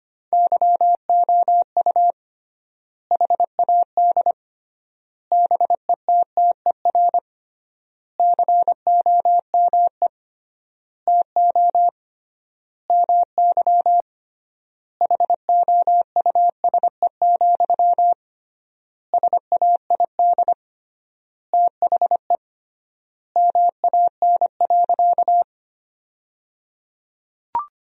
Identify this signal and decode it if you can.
Morse code, heard as a keyed tone, about 25 wpm, 700 Hz. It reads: YOU HAD BETTER COME TO MY HOUSE, SAID THE MAN.
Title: 3x Word-Spacing - Sentences from Top 400 Words - 25wpm